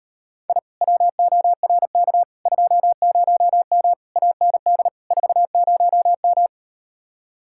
Morse code, heard as a keyed tone, about 38 wpm, 700 Hz. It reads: I WORK 20M AND 40M